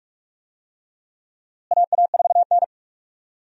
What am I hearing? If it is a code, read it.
Morse code, heard as a keyed tone, about 42 wpm, 700 Hz. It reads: AA4N
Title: USA Call Signs - 4 Characters - Repeated 3x - 45wpm